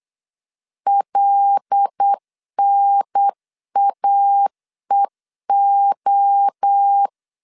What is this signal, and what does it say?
Morse code, heard as a keyed tone, about 8 wpm, 800 Hz. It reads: LNAEO
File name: LEZIONI / 7 GF9Z / 7 G 8wpm 3118